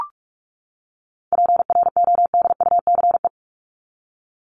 Morse code, heard as a keyed tone, about 44 wpm, 700 Hz. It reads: PRODUCE
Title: Top 500 Words 45wpm